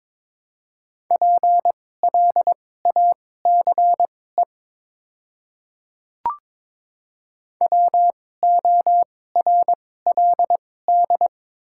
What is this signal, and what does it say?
Morse code, heard as a keyed tone, about 22 wpm, spaced as for 16 wpm, 700 Hz. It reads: PLACE WORLD
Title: 2x Character Spacing - Top 200 Words - 22wpm